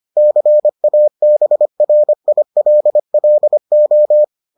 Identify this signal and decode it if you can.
Morse code, heard as a keyed tone, about 25 wpm, 600 Hz. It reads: CABRILLO